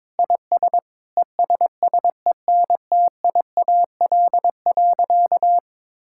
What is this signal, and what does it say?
Morse code, heard as a keyed tone, about 22 wpm, 700 Hz. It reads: IS ESSENTIAL.
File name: Sentences with 7-Word Phrases - 22wpm